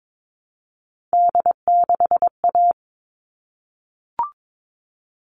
Morse code, heard as a keyed tone, about 22 wpm, 700 Hz. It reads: D6A
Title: All Cut Numbers - 3-Digit 22wpm